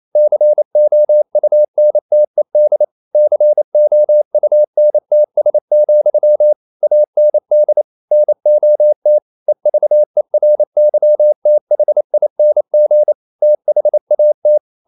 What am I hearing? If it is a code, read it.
Morse code, heard as a keyed tone, about 28 wpm, 600 Hz. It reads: COUNTED COUNTS, AND NOT EVERYTHING THAT
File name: CW / Practice / EnglishQuotations0128